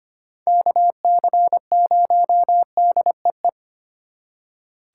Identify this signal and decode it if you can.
Morse code, heard as a keyed tone, about 29 wpm, 700 Hz. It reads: KC0DEE
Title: Speed Racing - USA 2x3 Call Signs 25wpm